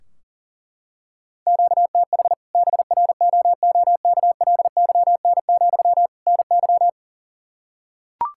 Morse code, heard as a keyed tone, about 37 wpm, 700 Hz. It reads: QTH BROOKLYN, NY
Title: Practice QSO - Exchange 1 - Beginning - 40wpm